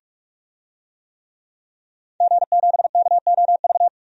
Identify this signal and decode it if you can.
Morse code, heard as a keyed tone, about 46 wpm, 700 Hz. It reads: G7KOV
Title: DXCC-100 Call Signs - 5 Characters - 45wpm